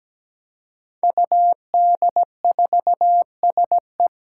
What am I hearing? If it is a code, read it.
Morse code, heard as a keyed tone, about 17 wpm, 700 Hz. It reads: UD4SE
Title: Five Challenging Letter-Number Combinations 17wpm